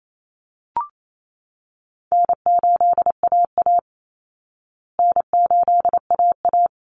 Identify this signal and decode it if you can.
Morse code, heard as a keyed tone, about 28 wpm, 700 Hz. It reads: N8AA N8AA
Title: Call Signs Repeated 3x - 28wpm